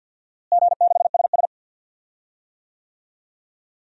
Morse code, heard as a keyed tone, about 50 wpm, 700 Hz. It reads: G6SS
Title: DXCC-100 Call Signs - 4 Characters - Repeated 3x - 50wpm